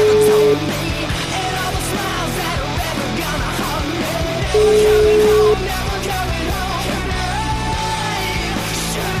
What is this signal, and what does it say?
Ringback tone (Europe, 425 Hz), with noise.